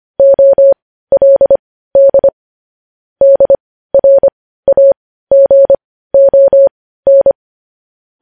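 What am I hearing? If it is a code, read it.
Morse code, heard as a keyed tone, about 25 wpm, spaced as for 15 wpm, 550 Hz. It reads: OLD DRAGON
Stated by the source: Morse practice of news headlines at 15 wpm.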